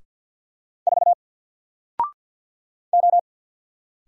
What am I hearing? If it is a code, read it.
Morse code, heard as a keyed tone, about 48 wpm, 700 Hz. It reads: 4 O
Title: Single Letter-Number Alt 50wpm